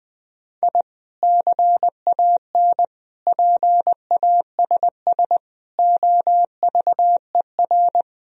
Morse code, heard as a keyed tone, about 21 wpm, 700 Hz. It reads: I CAN PASS OVER
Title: Sentences from Top 1000 Words - Review Part 1 - 20wpm